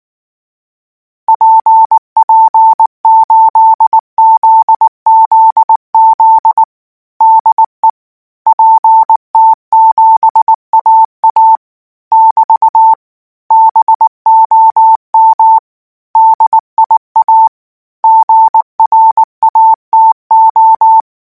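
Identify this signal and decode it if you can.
Morse code, heard as a keyed tone, about 19 wpm, 890 Hz. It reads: PP8ZZZ DE PT7AA <BT> BOM DIA GRATO
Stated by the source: Morse code practice text at 18 wpm, from Portal do CW.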